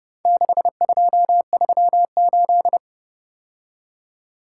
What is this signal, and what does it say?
Morse code, heard as a keyed tone, about 30 wpm, 700 Hz. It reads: 6238